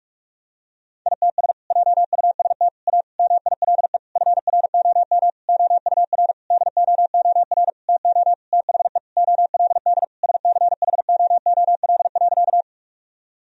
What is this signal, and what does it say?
Morse code, heard as a keyed tone, about 46 wpm, 700 Hz. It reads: ITS JUST A MILE FROM OUR DOOR TO THE OLD SCHOOL.
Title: Sentences from Top 1000 Words - Review Part 1 - 45wpm